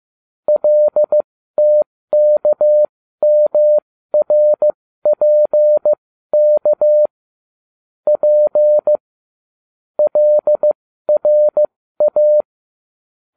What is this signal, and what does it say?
Morse code, heard as a keyed tone, about 15 wpm, spaced as for 13 wpm, 600 Hz. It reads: LTKMRPK P LRA